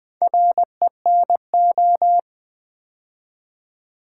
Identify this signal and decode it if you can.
Morse code, heard as a keyed tone, about 20 wpm, 700 Hz. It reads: RENO
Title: Build A Word - 2 Letters At A Time - 8-Letter Words - 20wpm